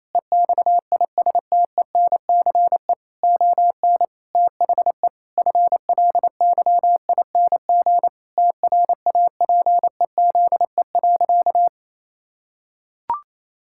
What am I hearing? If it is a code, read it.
Morse code, heard as a keyed tone, about 27 wpm, 700 Hz. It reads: EXISTENCE ON THE FLYING TRAPEZE.
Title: Pangram Challenge v2 - 28wpm